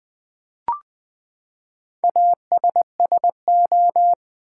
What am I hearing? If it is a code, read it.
Morse code, heard as a keyed tone, about 20 wpm, 700 Hz. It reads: ASSO
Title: Build A Word - 4 Letters At A Time - 10-Letter Words - 20wpm